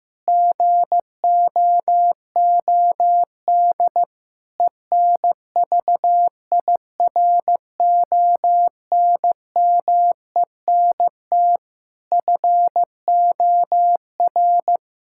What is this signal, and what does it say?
Morse code, heard as a keyed tone, about 14 wpm, 700 Hz. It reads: GOOD ENVIRONMENT FOR
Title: Sentences from Top 1000 Words - Review Part 5 - 15wpm